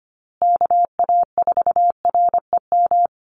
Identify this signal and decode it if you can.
Morse code, heard as a keyed tone, about 25 wpm, 700 Hz. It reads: KA4REM
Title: USA Call Signs Repeated 2x 25wpm